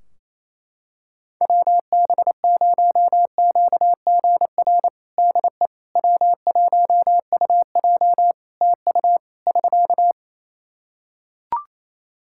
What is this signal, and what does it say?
Morse code, heard as a keyed tone, about 27 wpm, 700 Hz. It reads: WB0QGR DE W1UJ TU <SK>
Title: Practice QSO - Exchange 4 - Ending - 28wpm